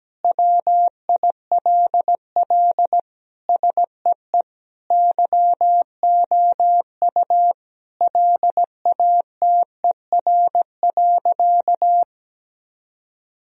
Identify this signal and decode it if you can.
Morse code, heard as a keyed tone, about 17 wpm, 700 Hz. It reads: WILL SEE YOU LATER.
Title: Sentences from Top 1000 Words - Review Part 4 - 17wpm